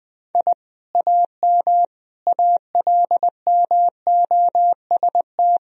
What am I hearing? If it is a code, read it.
Morse code, heard as a keyed tone, about 21 wpm, 700 Hz. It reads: I AM ALMOST
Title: Sentences from Top 1000 Words - Review Part 4 - 20wpm